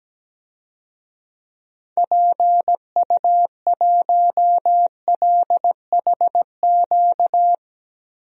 Morse code, heard as a keyed tone, about 17 wpm, 700 Hz. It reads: PU1LHQ
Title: DXCC-100 Call Signs - 6 Characters - 17wpm